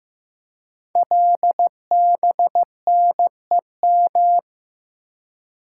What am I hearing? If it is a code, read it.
Morse code, heard as a keyed tone, about 15 wpm, 700 Hz. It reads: LBNEM
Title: Code Groups - 5 Letters - 15wpm